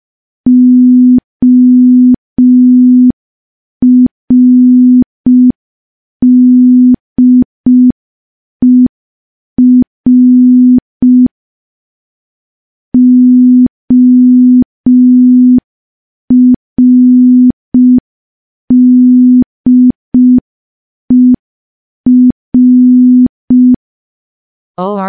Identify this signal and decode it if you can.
Morse code, heard as a keyed tone, about 5 wpm, 250 Hz. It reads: ORDER ORDER
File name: morse / morse-training-05WPM